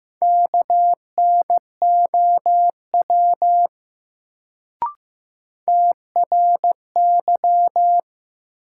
Morse code, heard as a keyed tone, about 15 wpm, 700 Hz. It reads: KNOW TRY
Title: Sets of 2 Words - New Word Drill - 15wpm